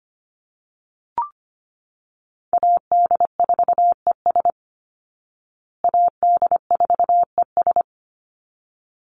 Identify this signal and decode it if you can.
Morse code, heard as a keyed tone, about 25 wpm, 700 Hz. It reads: AD4ES AD4ES
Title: Call Signs Repeated 3x - 25wpm